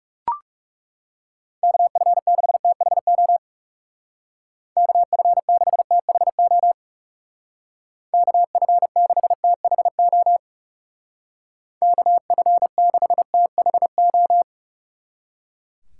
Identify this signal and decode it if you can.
Morse code, heard as a keyed tone, about 36 wpm, 700 Hz. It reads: KF6THO KF6THO KF6THO KF6THO
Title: Speed Racing - USA Call Signs 30wpm